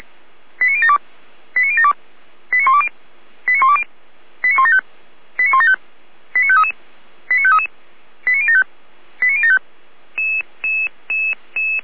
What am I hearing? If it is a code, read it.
Five-tone selective call (ZVEI), heard as quick runs of five tones, tones of about 70 ms. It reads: 89072 89072 89120 89120 89175 89175 89533 89533 89075 89075